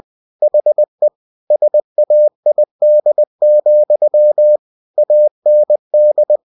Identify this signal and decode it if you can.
Morse code, heard as a keyed tone, about 20 wpm, 600 Hz. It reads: HE SAID, AND
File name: cw-books / winnie-the-pooh / winnie-20wpm-page-0013